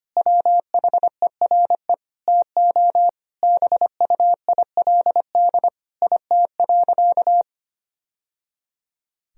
Morse code, heard as a keyed tone, about 25 wpm, 700 Hz. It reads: WHERE TO BUILD IT.
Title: Sentences from Top 1000 Words - Review Part 1 - 25wpm